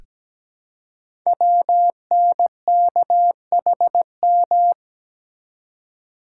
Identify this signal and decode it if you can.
Morse code, heard as a keyed tone, about 17 wpm, 700 Hz. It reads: WNKHM
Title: Code Groups - 5 Letters - 17wpm